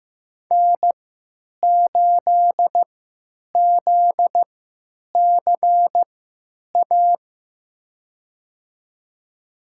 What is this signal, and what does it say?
Morse code, heard as a keyed tone, about 15 wpm, spaced as for 9 wpm, 700 Hz. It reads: N8ZCA
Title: USA Call Signs - 5 Characters - 3x Character Spacing - 15wpm